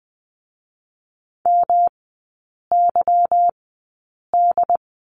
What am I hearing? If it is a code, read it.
Morse code, heard as a keyed tone, about 20 wpm, spaced as for 8 wpm, 700 Hz. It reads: MYD
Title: 152 Lesson 25e - Letter p - Three Characters 20wpm